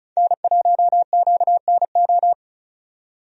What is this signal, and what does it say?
Morse code, heard as a keyed tone, about 34 wpm, 700 Hz. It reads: N1QNO
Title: USA Call Signs - 5 Characters - 35wpm